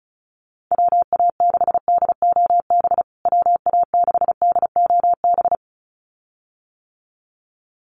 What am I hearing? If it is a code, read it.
Morse code, heard as a keyed tone, about 34 wpm, 700 Hz. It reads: WA6DOB WA6DOB
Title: USA Call Signs Repeated 2x 35wpm